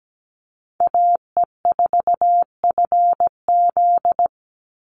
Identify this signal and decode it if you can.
Morse code, heard as a keyed tone, about 17 wpm, 700 Hz. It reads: AE4FZ
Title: NA SOTA Chaser Callsigns 17wpm